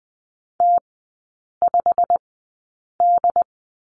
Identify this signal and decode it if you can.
Morse code, heard as a keyed tone, about 20 wpm, spaced as for 8 wpm, 690 Hz. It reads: T5D
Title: 087 Lesson 15e - Number 5 - Three Characters 20wpm